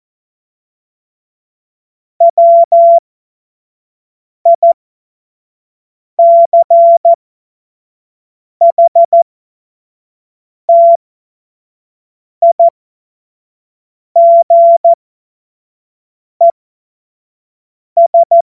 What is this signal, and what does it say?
Morse code, heard as a keyed tone, about 14 wpm, spaced as for 5 wpm, 680 Hz. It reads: WICHTIGES
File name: Morsekurs / Witze / Witz 20. Juli 2021, 25 F70